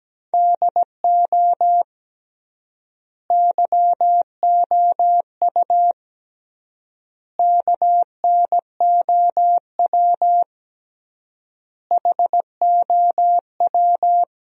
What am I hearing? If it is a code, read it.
Morse code, heard as a keyed tone, about 17 wpm, 700 Hz. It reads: DO YOU KNOW HOW
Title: 3x Word-Spacing - Sentences from Top 300 Words - 17wpm